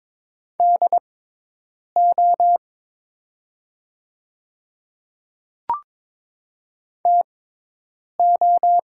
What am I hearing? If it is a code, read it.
Morse code, heard as a keyed tone, about 22 wpm, spaced as for 8 wpm, 700 Hz. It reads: DO TO
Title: 6x Character Spacing - Top 100 Words - 22wpm